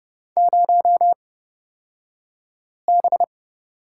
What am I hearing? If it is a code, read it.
Morse code, heard as a keyed tone, about 29 wpm, spaced as for 5 wpm, 700 Hz. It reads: 0B
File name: Lesson 37d - Letter x - Two Characters 30wpm